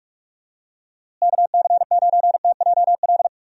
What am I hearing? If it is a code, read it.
Morse code, heard as a keyed tone, about 46 wpm, 700 Hz. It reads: KC9TJL
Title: USA Call Signs - 6 Characters - Repeated 3x - 45wpm